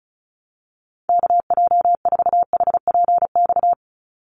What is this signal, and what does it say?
Morse code, heard as a keyed tone, about 34 wpm, 700 Hz. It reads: KJ4HPX
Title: USA Call Signs 35wpm